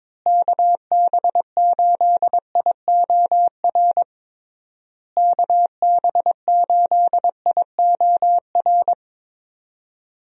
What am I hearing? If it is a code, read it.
Morse code, heard as a keyed tone, about 22 wpm, 700 Hz. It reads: KB8IOR KB8IOR
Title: USA Call Signs - 6 Characters - Repeated 3x - 22wpm